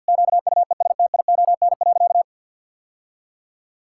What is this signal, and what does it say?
Morse code, heard as a keyed tone, about 51 wpm, 700 Hz. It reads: QUESTION.
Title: Sentences from Top 1000 Words - Review Part 5 - 50wpm